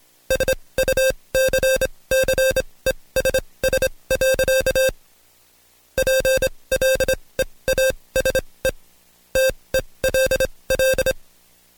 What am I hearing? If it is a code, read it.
Morse code, heard as a keyed tone, about 26 wpm, spaced as for 20 wpm, 530 Hz. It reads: SUCCESS. PLEASE TELL
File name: Code Quick / AudioDownld2 / Seg136-Stepper9 19WPM